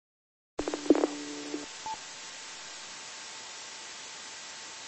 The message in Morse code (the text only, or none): none